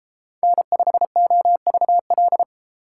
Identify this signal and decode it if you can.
Morse code, heard as a keyed tone, about 33 wpm, 690 Hz. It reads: N5OVL